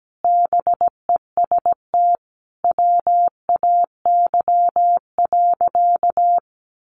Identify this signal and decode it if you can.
Morse code, heard as a keyed tone, about 17 wpm, 700 Hz. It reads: BEST WAY.
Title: Sentences from Top 400 Words 17wpm